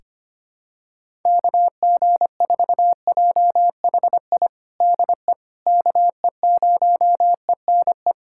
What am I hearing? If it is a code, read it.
Morse code, heard as a keyed tone, about 25 wpm, 700 Hz. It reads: KG4JHI DE KE0ENE
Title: USA Callsign DE Callsign 25wpm